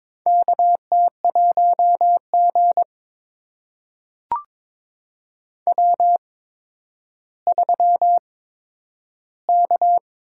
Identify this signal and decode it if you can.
Morse code, heard as a keyed tone, about 22 wpm, spaced as for 11 wpm, 700 Hz. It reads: KT1G W3K
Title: USA Call Signs - 4 Characters - 8x Character Spacing - 22wpm